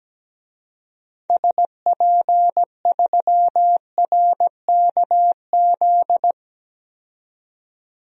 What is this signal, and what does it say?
Morse code, heard as a keyed tone, about 17 wpm, 700 Hz. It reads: SP3RKZ
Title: DXCC-100 Call Signs - 6 Characters - 17wpm